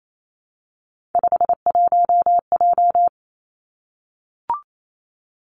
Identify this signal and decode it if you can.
Morse code, heard as a keyed tone, about 28 wpm, 700 Hz. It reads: 51J
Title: Three Challenging Letter-Number Combinations 28wpm